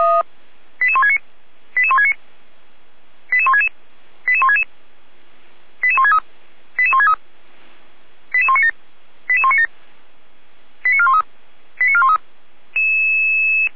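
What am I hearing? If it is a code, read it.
Five-tone selective call (ZVEI), heard as quick runs of five tones, tones of about 70 ms. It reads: 88169 88169 88166 88166 88163 88163 88197 88197 89413 89413
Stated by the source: Received via SDR; ZVEI five-tone call 88352.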